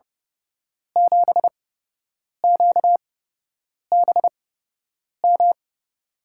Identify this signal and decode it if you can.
Morse code, heard as a keyed tone, about 31 wpm, spaced as for 8 wpm, 700 Hz. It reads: 7QBM
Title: USA Call Signs - 5 Characters - 8x Character Spacing - 30wpm